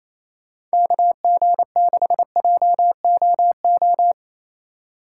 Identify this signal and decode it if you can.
Morse code, heard as a keyed tone, about 35 wpm, 700 Hz. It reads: KG6JOO